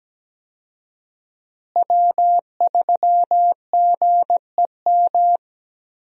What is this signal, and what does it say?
Morse code, heard as a keyed tone, about 17 wpm, 700 Hz. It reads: W3GEM
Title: USA Call Signs - 5 Characters - Repeated 2x - 17wpm